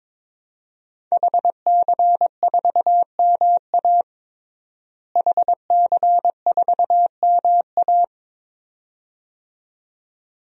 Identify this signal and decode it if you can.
Morse code, heard as a keyed tone, about 22 wpm, 700 Hz. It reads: HC4MA HC4MA
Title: DXCC-100 Call Signs - 5 Characters - Repeated 2x - 22wpm